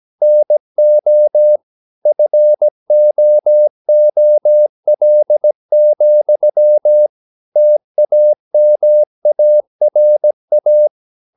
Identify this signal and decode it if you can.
Morse code, heard as a keyed tone, about 17 wpm, 600 Hz. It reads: NO FOOL, TAMARA
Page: Romanen Never af Ken Follet på engelsk.